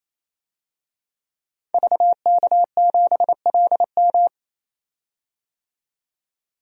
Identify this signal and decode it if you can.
Morse code, heard as a keyed tone, about 28 wpm, 700 Hz. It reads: VK7LM
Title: DXCC-100 Call Signs - 5 Characters - 28wpm